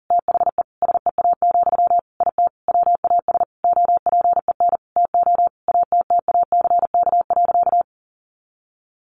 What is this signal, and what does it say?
Morse code, heard as a keyed tone, about 40 wpm, 700 Hz. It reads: THE SEA, IT WAS OPEN TO ATTACK.
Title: Sentences from Top 1500 Words 40wpm